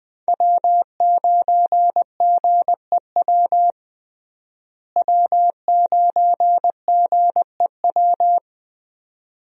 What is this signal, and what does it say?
Morse code, heard as a keyed tone, about 20 wpm, 700 Hz. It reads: W9GEW W9GEW
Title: USA Call Signs - 5 Characters - Repeated 2x - 20wpm